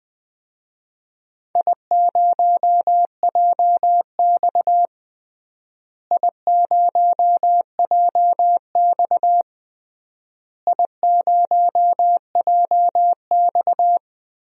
Morse code, heard as a keyed tone, about 21 wpm, 700 Hz. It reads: I0JX I0JX I0JX
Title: DXCC-100 Call Signs - 4 Characters - Repeated 3x - 20wpm